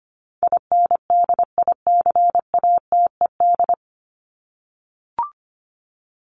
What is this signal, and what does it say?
Morse code, heard as a keyed tone, about 25 wpm, 700 Hz. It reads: INDICATED
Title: ED Suffix - Top 100 Words with Base Words 25wpm